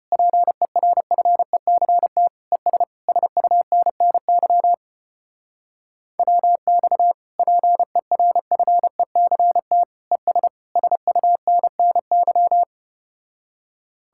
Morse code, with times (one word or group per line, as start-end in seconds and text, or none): 0.1-2.3 PERFECT
2.5-2.8 ES
3.1-4.7 SUNNY
6.2-7.1 WX
7.4-9.8 PERFECT
10.1-10.5 ES
10.8-12.6 SUNNY